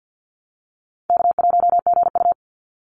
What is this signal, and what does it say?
Morse code, heard as a keyed tone, about 50 wpm, 700 Hz. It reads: K1GU
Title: Call Signs - 50wpm